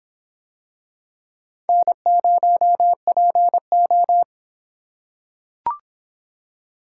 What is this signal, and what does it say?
Morse code, heard as a keyed tone, about 23 wpm, 700 Hz. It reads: N0PO